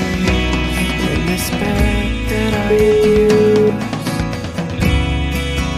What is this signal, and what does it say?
Ringback tone (Europe, 425 Hz), with noise.